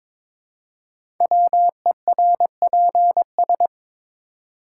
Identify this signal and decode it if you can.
Morse code, heard as a keyed tone, about 23 wpm, 700 Hz. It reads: WERPS
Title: Code Groups - 5 Letters - 22wpm